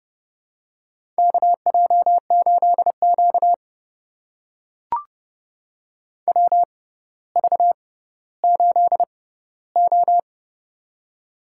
Morse code, heard as a keyed tone, about 31 wpm, spaced as for 16 wpm, 700 Hz. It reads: KJ8Q WV8O
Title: USA Call Signs - 4 Characters - 6x Character Spacing - 30wpm